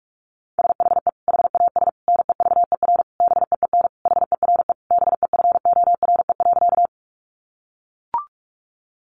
Morse code, heard as a keyed tone, about 44 wpm, 700 Hz. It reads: SHE HAS NEVER BEEN HERE BEFORE.
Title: Additional Easy Phrases 45wpm